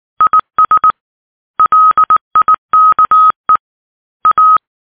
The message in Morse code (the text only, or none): IS LIKE A